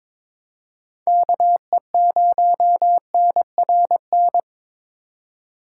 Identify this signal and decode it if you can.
Morse code, heard as a keyed tone, about 22 wpm, 700 Hz. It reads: KE0NRN
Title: USA Call Signs - 4x Character Spacing - 22wpm